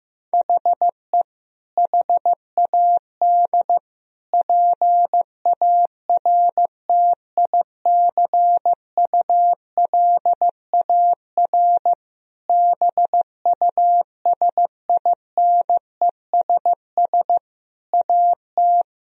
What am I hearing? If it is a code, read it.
Morse code, heard as a keyed tone, about 15 wpm, 700 Hz. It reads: HE HAD PARTICULAR BUSINESS AT
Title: Sentences from Top 1000 Words - Review Part 4 - 15wpm